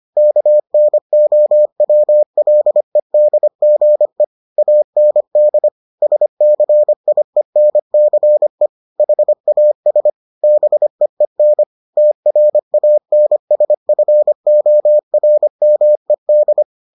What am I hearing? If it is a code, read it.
Morse code, heard as a keyed tone, about 25 wpm, 600 Hz. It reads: KNOWLEDGE AND SCIENCE HAS BEEN TRANSFORMED